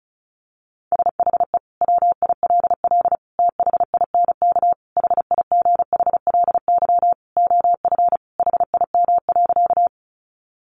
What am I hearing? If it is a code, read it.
Morse code, heard as a keyed tone, about 34 wpm, 700 Hz. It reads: SHE WILL THINK HIGHLY OF HIM.
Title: Sentences from Top 1100 Words 35wpm